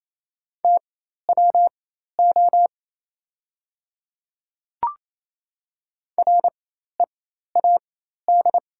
Morse code, heard as a keyed tone, about 28 wpm, spaced as for 13 wpm, 690 Hz. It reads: TWO READ
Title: 4x Character Spacing - Top 200 Words - 28wpm